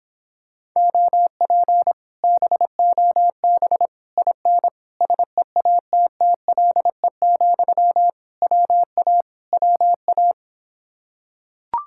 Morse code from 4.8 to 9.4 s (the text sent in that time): SEATTLE, WA